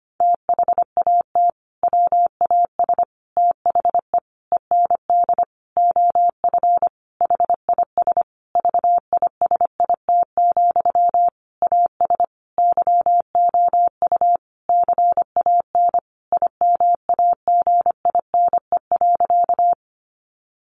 Morse code, heard as a keyed tone, about 25 wpm, 700 Hz. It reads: THAT WAS THE END OF HIS VISIT, AS YOU CAN IMAGINE.
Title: Sentences from Top 1400 Words 25wpm